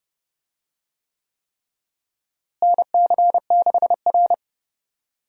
Morse code, heard as a keyed tone, about 31 wpm, 700 Hz. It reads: NC6R
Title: USA Call Signs - 4 Characters - Repeated 3x - 30wpm